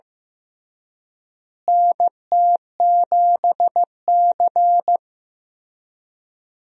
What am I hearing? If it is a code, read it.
Morse code, heard as a keyed tone, about 15 wpm, 700 Hz. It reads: NT7C
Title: USA Call Signs - 4 Characters - Repeated 3x - 15wpm